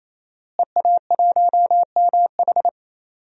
Morse code, heard as a keyed tone, about 27 wpm, 700 Hz. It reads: EA1MH
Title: DXCC-100 Call Signs - 5 Characters - Repeated 3x - 28wpm